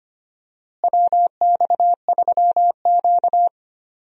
Morse code, heard as a keyed tone, about 25 wpm, 700 Hz. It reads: WX3Q
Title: USA Call Signs - 4 Characters - Repeated 2x - 25wpm